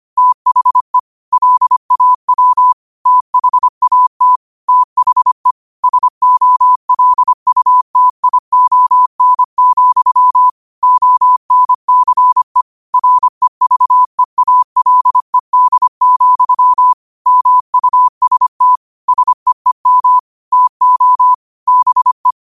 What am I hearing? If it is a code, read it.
Morse code, heard as a keyed tone, about 25 wpm, 1000 Hz. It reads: THE LAW THAT THE SOLUTION, ONCE REVEALED, MUST SEEM TO BE
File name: qotdfiles / 25wpm